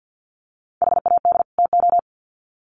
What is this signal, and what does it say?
Morse code, heard as a keyed tone, about 50 wpm, 700 Hz. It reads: HAD TO
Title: Sets of 2 Words 50wpm